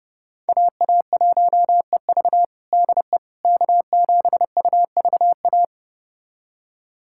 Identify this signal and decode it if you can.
Morse code, heard as a keyed tone, about 32 wpm, 700 Hz. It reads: AA1EV DE K7UVA
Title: USA Callsign DE Callsign - 5 Characters - 30wpm